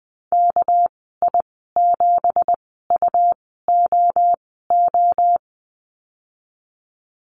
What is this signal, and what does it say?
Morse code, heard as a keyed tone, about 20 wpm, spaced as for 15 wpm, 700 Hz. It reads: KI7UOO
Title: 200 Lesson 32g - Letter k - Call Signs 20wpm